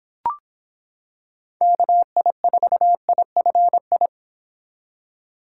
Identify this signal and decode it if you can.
Morse code, heard as a keyed tone, about 26 wpm, 700 Hz. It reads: KI4IFI